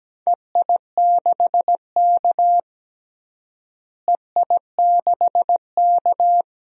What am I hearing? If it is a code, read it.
Morse code, heard as a keyed tone, about 17 wpm, 700 Hz. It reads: EI6K EI6K
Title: DXCC-100 Call Signs - 4 Characters - Repeated 2x - 17wpm